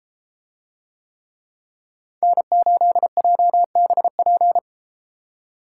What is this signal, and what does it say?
Morse code, heard as a keyed tone, about 32 wpm, 700 Hz. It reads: N8JBP